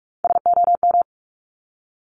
Morse code, heard as a keyed tone, about 46 wpm, 690 Hz. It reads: SOM
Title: Top Three Letter Combinations in Words 45wpm